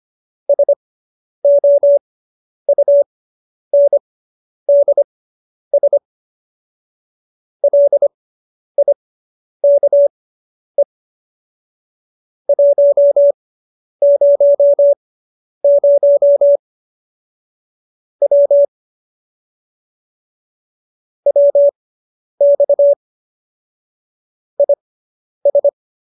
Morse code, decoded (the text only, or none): SOUNDS LIKE 100 W WX IS